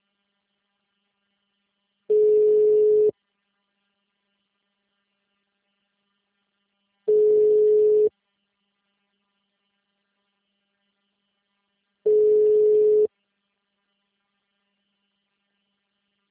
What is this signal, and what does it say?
Ringback tone (Europe, 425 Hz).